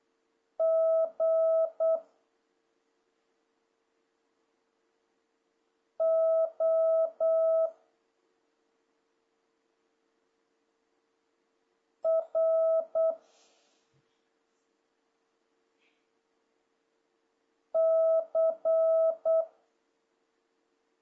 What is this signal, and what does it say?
Morse code, heard as a keyed tone, about 8 wpm, 650 Hz. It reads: G O R C
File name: jh8cbh / 2026cw